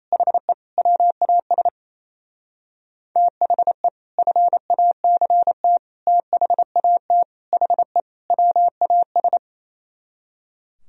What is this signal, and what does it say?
Morse code, heard as a keyed tone, about 35 wpm, 700 Hz. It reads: HE WAS THE FACT THAT HE WAS
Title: Speed-Racing - Sets of 5 Words 28wpm